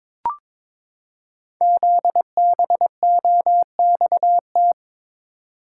Morse code, heard as a keyed tone, about 22 wpm, 700 Hz. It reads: ZBOXT
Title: Code Groups - 5 Letters - 22wpm